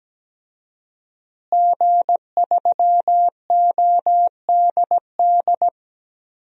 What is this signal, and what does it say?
Morse code, heard as a keyed tone, about 17 wpm, 700 Hz. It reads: G3ODD
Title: DXCC-100 Call Signs - 5 Characters - Repeated 3x - 17wpm